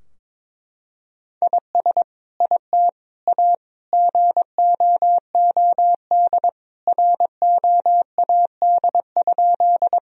Morse code, heard as a keyed tone, about 22 wpm, 700 Hz. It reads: IS IT A GOOD ROAD?
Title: Sentences from Top 1000 Words - Review Part 2 - 22wpm